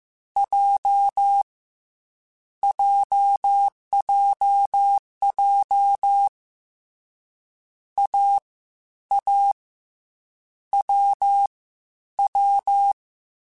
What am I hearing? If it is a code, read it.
Morse code, heard as a keyed tone, about 15 wpm, 800 Hz. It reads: J JJJ A A W W
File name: Difficult Combos / Difficult Combos-mp3s / AWJ ShortRun @13